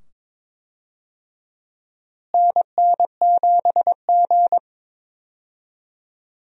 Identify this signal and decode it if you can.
Morse code, heard as a keyed tone, about 21 wpm, 700 Hz. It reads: NN7G
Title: USA Call Signs - 4 Characters - Repeated 2x - 22wpm